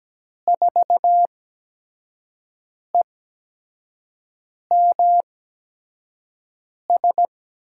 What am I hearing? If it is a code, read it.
Morse code, heard as a keyed tone, about 17 wpm, spaced as for 5 wpm, 700 Hz. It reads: 4EMS
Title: USA Call Signs - 6 Characters - 8x Character Spacing - 17wpm